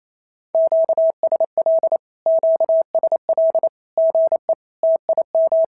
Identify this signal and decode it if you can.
Morse code, heard as a keyed tone, about 28 wpm, 650 Hz. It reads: QSL QSL GE TIM
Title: SOTA QSO Practice 28wpm